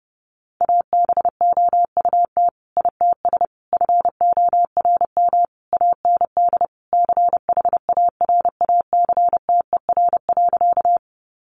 Morse code, heard as a keyed tone, about 30 wpm, 700 Hz. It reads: ABOUT ITS FORM AND CHARACTER.
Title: Sentences from Top 1200 Words 30wpm